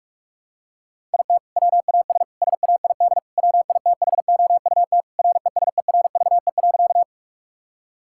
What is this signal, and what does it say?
Morse code, heard as a keyed tone, about 46 wpm, 700 Hz. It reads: IT WAS SAID WITHOUT RESERVE.
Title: Sentences from Top 1000 Words - Review Part 6 - 45wpm